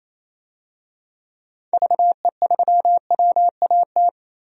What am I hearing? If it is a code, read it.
Morse code, heard as a keyed tone, about 28 wpm, 700 Hz. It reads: VE3WAT
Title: DXCC-100 Call Signs - 6 Characters - 28wpm